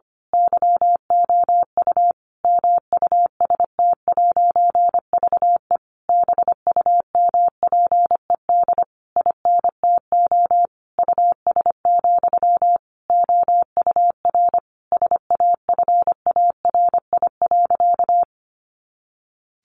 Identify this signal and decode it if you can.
Morse code, heard as a keyed tone, about 25 wpm, 700 Hz. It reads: YOU MUST'VE BUMPED INTO US, OUR SAFARI.
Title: A Sound Of Thunder 25wpm